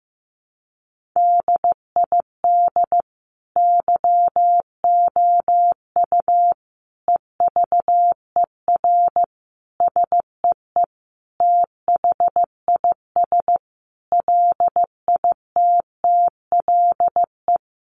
Morse code, heard as a keyed tone, about 15 wpm, 700 Hz. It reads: DID YOU EVER SEE THIS LITTLE
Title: Sentences from Top 1500 Words 15wpm